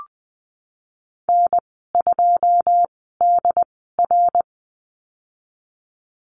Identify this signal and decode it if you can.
Morse code, heard as a keyed tone, about 20 wpm, spaced as for 15 wpm, 700 Hz. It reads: N2DR
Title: 088 Lesson 15g - Number 5 - Call Signs 20wpm